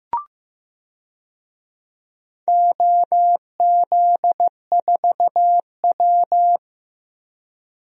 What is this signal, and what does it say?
Morse code, heard as a keyed tone, about 16 wpm, 700 Hz. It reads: OZ4W
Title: DXCC-100 Call Signs - 4 Characters - 15wpm